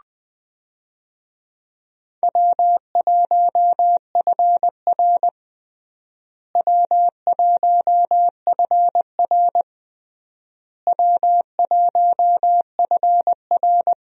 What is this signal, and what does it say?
Morse code, heard as a keyed tone, about 20 wpm, 700 Hz. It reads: W1FR W1FR W1FR
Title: USA Call Signs - 4 Characters - Repeated 3x - 20wpm